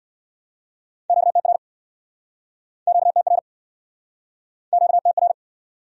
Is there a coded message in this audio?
Yes: Morse code from 1.1 to 5.3 s, heard as a keyed tone, about 70 wpm, 700 Hz. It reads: QTR QTR QTR